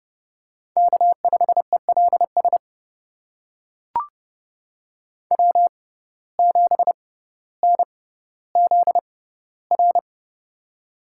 Morse code, heard as a keyed tone, about 30 wpm, spaced as for 15 wpm, 700 Hz. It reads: K5ELS W7NZR
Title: USA Call Signs - 5 Characters - 6x Character Spacing - 30wpm